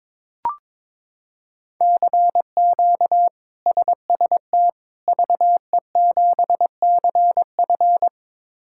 Morse code, heard as a keyed tone, about 22 wpm, 700 Hz. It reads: CQ SST VE7CF
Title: K1USN SST QSO Practice - 22wpm